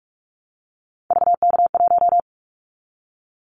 Morse code, heard as a keyed tone, about 46 wpm, 700 Hz. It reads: VK1
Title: Callsign Prefixes for Top 100 DXCC - 45wpm